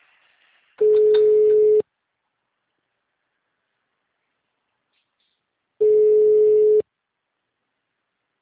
Ringback tone (Europe, 425 Hz).
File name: out- -2029- - - .76